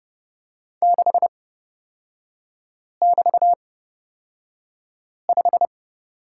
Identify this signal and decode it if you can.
Morse code, heard as a keyed tone, about 30 wpm, spaced as for 5 wpm, 700 Hz. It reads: 6<BT>5
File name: Lesson 36e - BT - Three Characters 30wpm